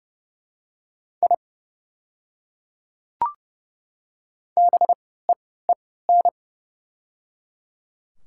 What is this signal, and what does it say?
Morse code, heard as a keyed tone, about 30 wpm, spaced as for 17 wpm, 700 Hz. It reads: I BEEN
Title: 3x Character Spacing - Top 100 Words - 30wpm